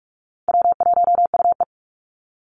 Morse code, heard as a keyed tone, about 44 wpm, 700 Hz. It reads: W1UE
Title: Call Signs - 45wpm